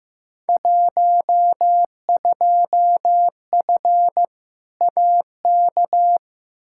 Morse code, heard as a keyed tone, about 15 wpm, 700 Hz. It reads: 12F AK
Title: Field Day Exchanges Encore - 15wpm